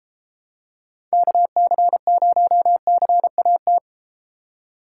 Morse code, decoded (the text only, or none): KC0CAT